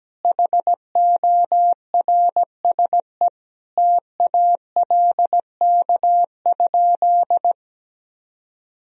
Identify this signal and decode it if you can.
Morse code, heard as a keyed tone, about 17 wpm, 700 Hz. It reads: HORSE TALK?
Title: Sentences from Top 1000 Words - Review Part 1 - 17wpm